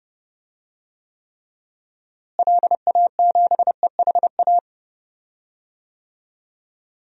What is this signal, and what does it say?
Morse code, heard as a keyed tone, about 32 wpm, 700 Hz. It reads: LA7EHA
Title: DXCC-100 Call Signs - 6 Characters - Repeated 3x - 30wpm